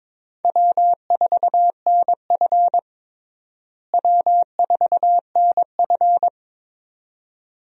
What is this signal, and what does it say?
Morse code, heard as a keyed tone, about 23 wpm, 700 Hz. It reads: W4NF W4NF
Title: USA Call Signs - 4 Characters - Repeated 2x - 22wpm